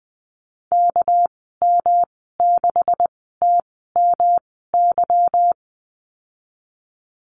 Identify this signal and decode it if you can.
Morse code, heard as a keyed tone, about 19 wpm, spaced as for 14 wpm, 700 Hz. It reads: KM6TMY
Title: 200 Lesson 32g - Letter k - Call Signs 20wpm